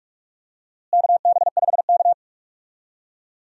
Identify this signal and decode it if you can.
Morse code, heard as a keyed tone, about 46 wpm, 700 Hz. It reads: KB5K
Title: USA Call Signs - 4 Characters - Repeated 3x - 45wpm